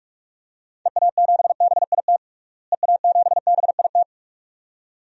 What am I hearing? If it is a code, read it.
Morse code, heard as a keyed tone, about 46 wpm, 700 Hz. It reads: EA7BIT EA7BIT
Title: DXCC-100 Call Signs - 6 Characters - Repeated 2x - 45wpm